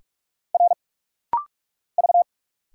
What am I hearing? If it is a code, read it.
Morse code, heard as a keyed tone, about 46 wpm, 700 Hz. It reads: R V
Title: Single Letters - Rapid-Fire 45wpm